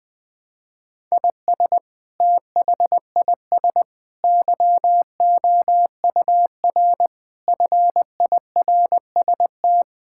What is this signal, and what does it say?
Morse code, heard as a keyed tone, about 20 wpm, 700 Hz. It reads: IS THIS YOUR FIRST
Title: Sentences from Top 1000 Words - Review Part 6 - 20wpm